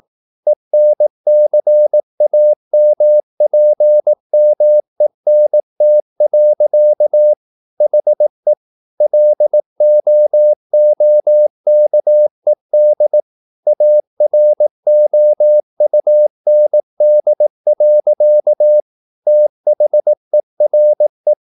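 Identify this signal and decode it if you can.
Morse code, heard as a keyed tone, about 18 wpm, 600 Hz. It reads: ENCAMPMENT. HE LOOKED AROUND. THERE
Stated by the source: Morse practice at 18 wpm, English text, from CW med Gnister.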